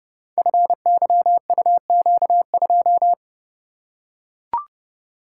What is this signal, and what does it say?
Morse code, heard as a keyed tone, about 30 wpm, 700 Hz. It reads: FYUQ2
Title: Code Groups - 5 Letter:Numbers - 30wpm